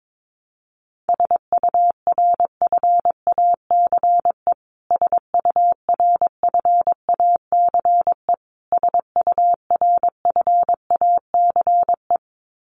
Morse code, heard as a keyed tone, about 22 wpm, 700 Hz. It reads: SURFACE SURFACE SURFACE
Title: Top 500 Words Repeated 22wpm